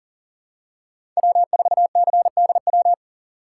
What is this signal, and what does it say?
Morse code, heard as a keyed tone, about 40 wpm, 700 Hz. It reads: W4CDW
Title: USA Call Signs - 5 Characters - 6x Character Spacing - 40wpm